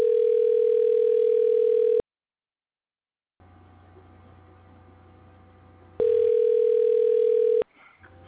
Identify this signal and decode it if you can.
Ringback tone (North America), 440 + 480 Hz.